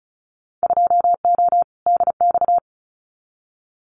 Morse code, heard as a keyed tone, about 34 wpm, 700 Hz. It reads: 2O DX
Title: Winter Field Day Exchanges Encore 35wpm